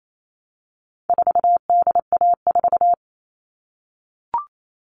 Morse code, heard as a keyed tone, about 28 wpm, 700 Hz. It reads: 4DA4
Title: All Cut Numbers - 4-Digit 28wpm